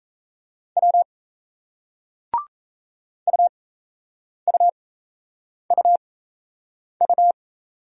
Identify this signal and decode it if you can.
Morse code, heard as a keyed tone, about 38 wpm, spaced as for 8 wpm, 700 Hz. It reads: W UUUU